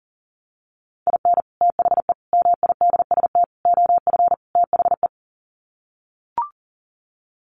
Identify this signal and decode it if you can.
Morse code, heard as a keyed tone, about 40 wpm, 700 Hz. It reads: IN THE MIDST OF THE
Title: Sets of 5 Words 40wpm